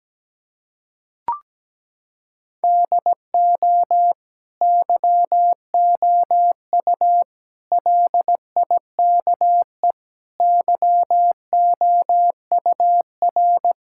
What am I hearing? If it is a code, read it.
Morse code, heard as a keyed tone, about 17 wpm, 700 Hz. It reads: DO YOU LIKE YOUR
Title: Sentences from Top 1000 Words - Review Part 5 - 17wpm